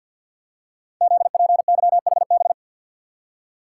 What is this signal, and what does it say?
Morse code, heard as a keyed tone, about 51 wpm, 700 Hz. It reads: ZPYHB
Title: Code Groups - 5 Letter:Numbers - 50wpm